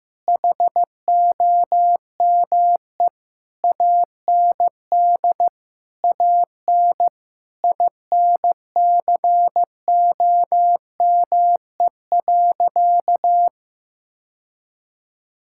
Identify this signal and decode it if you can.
Morse code, heard as a keyed tone, about 15 wpm, 700 Hz. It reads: HOME AND AN INCOME.
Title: Sentences from Top 1000 Words - Review Part 5 - 15wpm